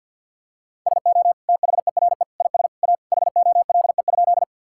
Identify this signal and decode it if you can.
Morse code, heard as a keyed tone, about 50 wpm, 700 Hz. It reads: SO THERE IS A HOLE?
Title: Sentences from Top 1600 Words 50wpm